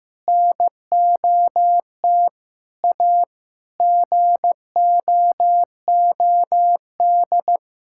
Morse code, heard as a keyed tone, about 16 wpm, 700 Hz. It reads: NOT A GOOD
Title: Sentences from Top 1600 Words 15wpm